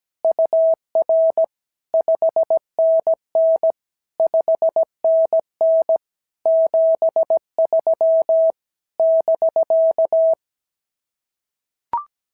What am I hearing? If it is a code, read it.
Morse code, heard as a keyed tone, about 17 wpm, 650 Hz. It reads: UR 5NN 5NN 73 <BK>
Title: SOTA QSO Practice 17wpm